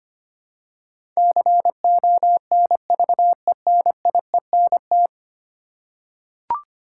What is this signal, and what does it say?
Morse code, heard as a keyed tone, about 25 wpm, 700 Hz. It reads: CONVENIENT
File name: 10 Letter Words - First 3 Letters Given - 25wpm